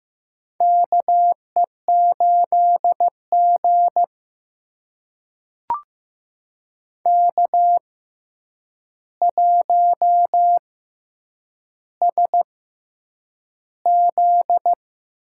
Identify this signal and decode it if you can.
Morse code, heard as a keyed tone, about 15 wpm, spaced as for 8 wpm, 700 Hz. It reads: KE8G K1SZ
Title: USA Call Signs - 4 Characters - 6x Character Spacing - 15wpm